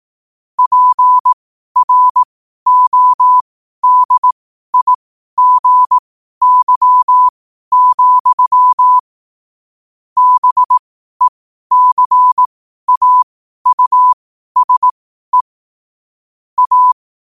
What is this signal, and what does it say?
Morse code, heard as a keyed tone, about 18 wpm, spaced as for 12 wpm, 1000 Hz. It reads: PRODIGY, BECAUSE A
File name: qotdfiles / 12wpm